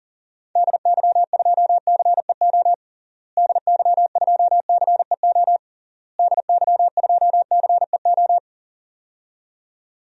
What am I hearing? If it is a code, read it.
Morse code, heard as a keyed tone, about 40 wpm, 700 Hz. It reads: DY2CEO DY2CEO DY2CEO
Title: DXCC-100 Call Signs - 6 Characters - Repeated 3x - 40wpm